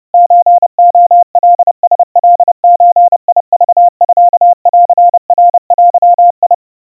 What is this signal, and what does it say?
Morse code (Japanese Wabun code), heard as a keyed tone, about 30 wpm, 700 Hz. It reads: ソレカラカゾクミンナデ
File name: 110 cpm 700 Hz Crispy